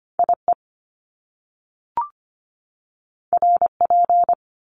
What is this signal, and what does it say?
Morse code, heard as a keyed tone, about 25 wpm, 700 Hz. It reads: IE RP
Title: Top Two Letter Combinations in Words 25wpm